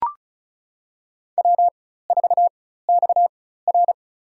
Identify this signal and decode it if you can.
Morse code, heard as a keyed tone, about 36 wpm, spaced as for 16 wpm, 700 Hz. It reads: W4XR
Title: USA Call Signs - 4 Characters - 4x Character Spacing - 35wpm